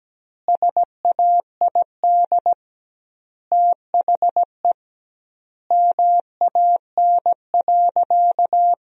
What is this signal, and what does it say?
Morse code, heard as a keyed tone, about 17 wpm, 700 Hz. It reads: SAID THE MAN.
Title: 2x Word-Spacing - Sentences from Top 400 Words - 17wpm